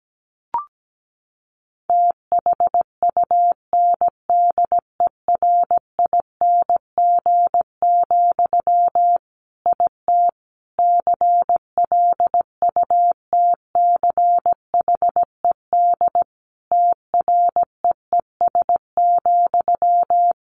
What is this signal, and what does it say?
Morse code, heard as a keyed tone, about 17 wpm, 700 Hz. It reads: THUNDERING, IT CLUTCHED TREES,
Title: A Sound Of Thunder 17wpm